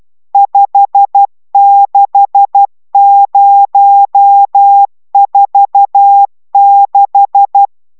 Morse code, heard as a keyed tone, about 12 wpm, 800 Hz. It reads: 56046